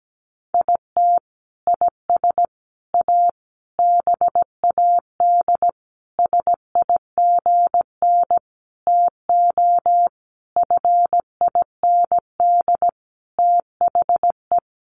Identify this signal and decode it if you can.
Morse code, heard as a keyed tone, about 17 wpm, 700 Hz. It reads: IT IS A BAD SIGN TO FIND THE
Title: Sentences from Top 1000 Words 17wpm